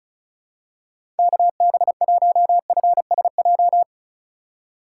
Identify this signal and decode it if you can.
Morse code, heard as a keyed tone, about 36 wpm, 700 Hz. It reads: KB1FSJ
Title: USA Call Signs - 6 Characters - 6x Character Spacing - 35wpm